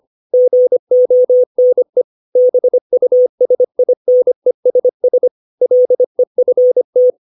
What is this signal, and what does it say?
Morse code, heard as a keyed tone, about 25 wpm, 500 Hz. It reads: GONE BUSINESS LEFT
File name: 632 words / 632 Words 25WPM